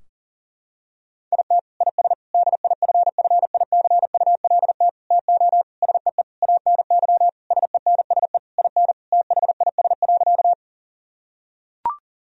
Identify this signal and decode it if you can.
Morse code, heard as a keyed tone, about 40 wpm, 700 Hz. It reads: IT IS DIFFICULT TO SEE ANY SENSE IN THIS.
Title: Sentences from Top 1000 Words - Review Part 4 - 40wpm